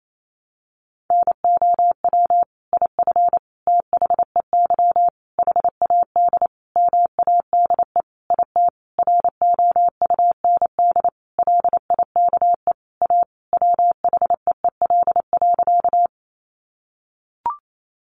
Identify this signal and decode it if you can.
Morse code, heard as a keyed tone, about 28 wpm, 700 Hz. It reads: NOW IF THEY HAD MADE IT ROUND LIKE A WHEEL.
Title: Sentences from Top 500 Words 28wpm